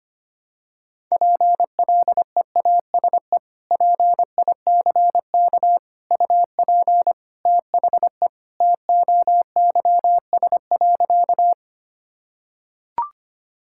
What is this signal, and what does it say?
Morse code, heard as a keyed tone, about 25 wpm, 700 Hz. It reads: PLEASE PICK UP THE TOYS.
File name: practice / 25wpm / Sentences with 2-Word Phrases - 25wpm